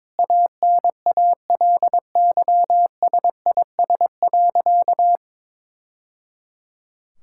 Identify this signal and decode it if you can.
Morse code, heard as a keyed tone, about 22 wpm, 700 Hz. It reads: ANALYSIS.
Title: Sentences from Top 1000 Words - Review Part 5 - 22wpm